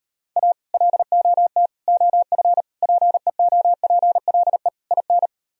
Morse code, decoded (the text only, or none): A LOT OF PEOPLE IN